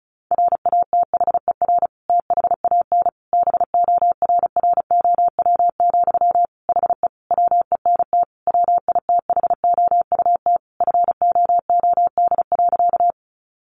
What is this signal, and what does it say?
Morse code, heard as a keyed tone, about 34 wpm, 700 Hz. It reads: RATHER THAN BORROW, HE WENT WITHOUT FOOD.
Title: Sentences from Top 1500 Words 35wpm